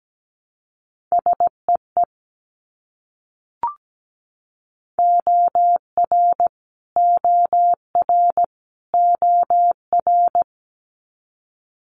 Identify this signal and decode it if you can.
Morse code, heard as a keyed tone, about 17 wpm, 700 Hz. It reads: SEE OR OR OR
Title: Multi-letter Phonemes Repeated 3x 17wpm